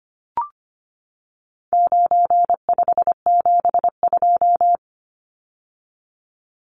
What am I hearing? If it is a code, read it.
Morse code, heard as a keyed tone, about 25 wpm, 700 Hz. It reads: 9572
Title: Numbers - 4-Digit 25wpm